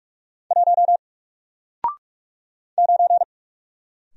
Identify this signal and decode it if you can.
Morse code, heard as a keyed tone, about 44 wpm, 700 Hz. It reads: 1 9
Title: Single Letter-Number Alt 45wpm